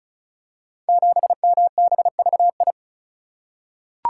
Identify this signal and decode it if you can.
Morse code, heard as a keyed tone, about 36 wpm, 700 Hz. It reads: 7MBVI
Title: Code Groups - 5 Letter:Numbers - 35wpm